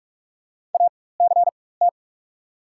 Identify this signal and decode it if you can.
Morse code, heard as a keyed tone, about 46 wpm, spaced as for 21 wpm, 700 Hz. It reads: ACT
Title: 4x Character Spacing - Top 300 Words - 45wpm